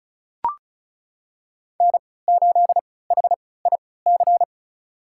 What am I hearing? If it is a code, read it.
Morse code, heard as a keyed tone, about 36 wpm, spaced as for 20 wpm, 700 Hz. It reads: N8HIC
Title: USA Call Signs - 5 Characters - 3x Character Spacing - 35wpm